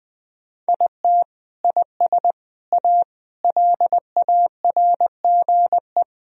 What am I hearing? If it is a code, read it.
Morse code, heard as a keyed tone, about 21 wpm, 700 Hz. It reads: IT IS A LARGE
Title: Sentences from Top 1000 Words - Review Part 4 - 20wpm